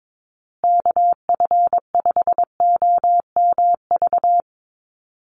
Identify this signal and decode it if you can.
Morse code, heard as a keyed tone, about 22 wpm, 700 Hz. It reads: KF5OMV
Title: USA Call Signs Repeated 2x 22wpm